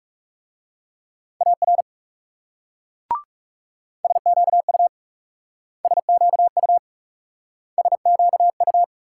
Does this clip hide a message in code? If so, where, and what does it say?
Yes: Morse code from 1.4 to 8.8 s, heard as a keyed tone, about 39 wpm, 700 Hz. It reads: AR SQU SQU SQU